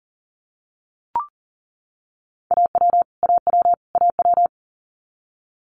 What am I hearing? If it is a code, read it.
Morse code, heard as a keyed tone, about 40 wpm, 700 Hz. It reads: AW AW AW
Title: Multi-letter Phonemes Repeated 3x 40wpm